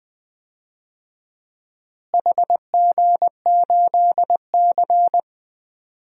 Morse code, heard as a keyed tone, about 20 wpm, 700 Hz. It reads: HG8C
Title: DXCC-100 Call Signs - 4 Characters - Repeated 3x - 20wpm